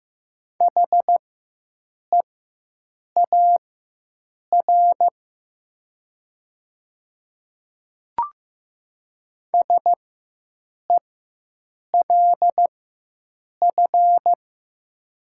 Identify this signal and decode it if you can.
Morse code, heard as a keyed tone, about 15 wpm, spaced as for 7 wpm, 700 Hz. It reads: HEAR SELF
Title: 4x Character Spacing - Top 300 Words - 15wpm